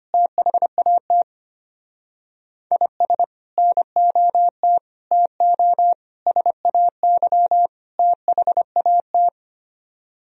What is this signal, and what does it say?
Morse code, heard as a keyed tone, about 31 wpm, 700 Hz. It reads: THAT IS NOT TO SAY THAT
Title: Speed-Racing - Sets of 5 Words 25wpm